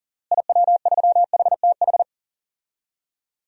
Morse code, heard as a keyed tone, about 37 wpm, 700 Hz. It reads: IW3HTH
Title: DXCC-100 Call Signs - 6 Characters - Repeated 3x - 40wpm